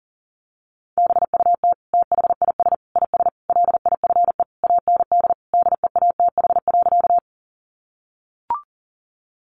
Morse code, heard as a keyed tone, about 37 wpm, 700 Hz. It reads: BUT THIS IS LIFE AND DEATH.
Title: Sentences from Top 1100 Words 40wpm